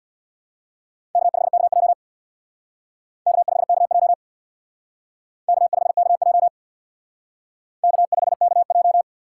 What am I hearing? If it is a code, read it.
Morse code, heard as a keyed tone, about 64 wpm, 700 Hz. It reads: K5KJ K5KJ K5KJ K5KJ